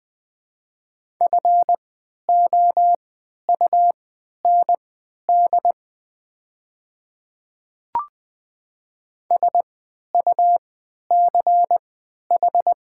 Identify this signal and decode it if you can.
Morse code, heard as a keyed tone, about 20 wpm, spaced as for 11 wpm, 700 Hz. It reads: FOUND SUCH
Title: 3x Character Spacing - Top 300 Words - 20wpm